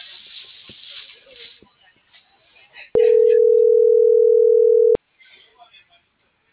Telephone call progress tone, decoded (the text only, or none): ringback tone